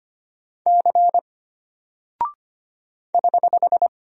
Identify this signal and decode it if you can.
Morse code, heard as a keyed tone, about 25 wpm, 700 Hz. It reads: C <HH>
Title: Single Character (Letter:Number:Punctuation:Prosign) Alt 25wpm